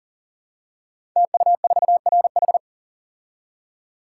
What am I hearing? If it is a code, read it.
Morse code, heard as a keyed tone, about 39 wpm, 700 Hz. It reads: TU4RH
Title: Code Groups - 5 Letter:Numbers - 40wpm